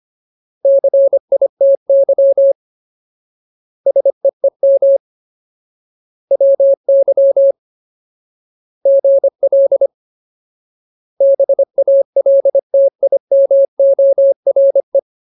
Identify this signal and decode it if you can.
Morse code, heard as a keyed tone, about 25 wpm, 550 Hz. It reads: CITY SEEM WY GL BALTIMORE
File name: Ultimate Word List / 25WPM Ultimate